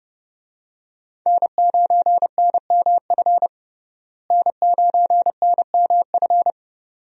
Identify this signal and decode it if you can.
Morse code, heard as a keyed tone, about 32 wpm, 700 Hz. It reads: N9NMF N9NMF
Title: USA Call Signs - 5 Characters - Repeated 2x - 30wpm